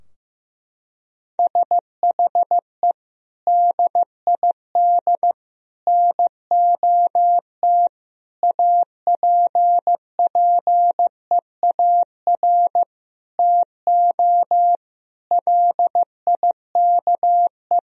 Morse code, heard as a keyed tone, about 15 wpm, 700 Hz. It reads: SHE DID NOT APPEAR TO LIKE
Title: Sentences from Top 1000 Words - Review Part 2 - 15wpm